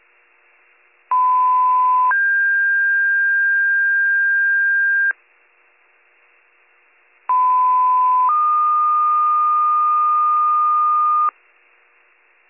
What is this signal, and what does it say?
Two-tone page, heard as a one-second tone, then a three-second tone, about 1010 then 1600 Hz, about 1010 then 1220 Hz. It is heard from 1.1 to 11.3 s.